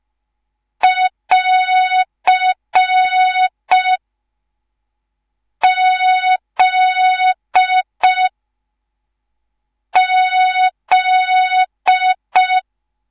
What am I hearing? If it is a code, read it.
Morse code, heard as a keyed tone, about 5 wpm, 3700 Hz. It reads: <AR> Z Z
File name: Morze-HA2MN / hang / 10GWZ HA2MN